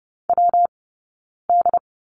Morse code, heard as a keyed tone, about 30 wpm, 700 Hz. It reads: W D
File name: Lesson 19d - Letter w - Two Characters 30wpm